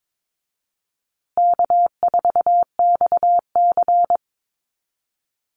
Morse code, heard as a keyed tone, about 22 wpm, 700 Hz. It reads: K4XC
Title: USA Call Signs Repeated 3x 22wpm